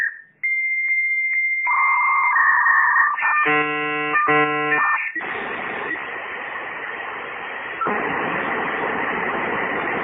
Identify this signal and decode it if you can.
Dial-up modem handshake.